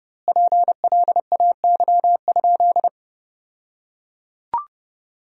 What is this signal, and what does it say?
Morse code, heard as a keyed tone, about 32 wpm, 700 Hz. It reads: PLAY?
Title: 2x Word-Spacing - Sentences from Top 300 Words - 30wpm